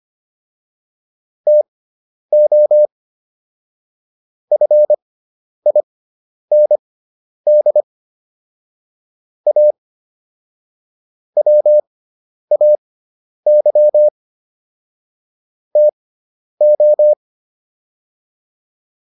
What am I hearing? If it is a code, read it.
Morse code, heard as a keyed tone, about 25 wpm, spaced as for 10 wpm, 600 Hz. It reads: TO FIND A WAY TO